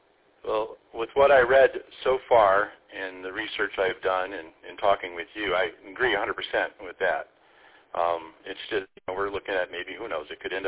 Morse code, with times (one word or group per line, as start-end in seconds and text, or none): none